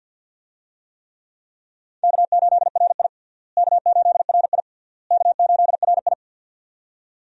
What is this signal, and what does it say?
Morse code, heard as a keyed tone, about 51 wpm, 700 Hz. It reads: K8RI K8RI K8RI
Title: USA Call Signs - 4 Characters - Repeated 3x - 50wpm